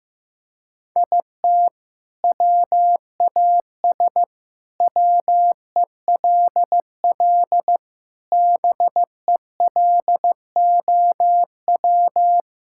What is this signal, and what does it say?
Morse code, heard as a keyed tone, about 15 wpm, 700 Hz. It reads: IT WAS WELL BELOW
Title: Sentences from Top 1000 Words - Review Part 6 - 15wpm